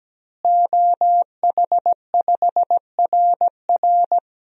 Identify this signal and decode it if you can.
Morse code, heard as a keyed tone, about 17 wpm, 700 Hz. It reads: OH5RR
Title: DXCC-100 Call Signs - 5 Characters - Repeated 3x - 17wpm